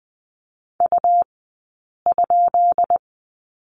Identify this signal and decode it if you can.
Morse code, heard as a keyed tone, about 20 wpm, 700 Hz. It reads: U ?
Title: 183 Lesson 30d - Letter b - Two Characters 20wpm